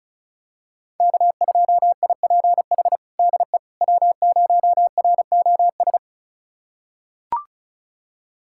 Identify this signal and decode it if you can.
Morse code, heard as a keyed tone, about 36 wpm, 700 Hz. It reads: K2IPH DE W0ROS
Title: USA Callsign DE Callsign - 5 Characters - 35wpm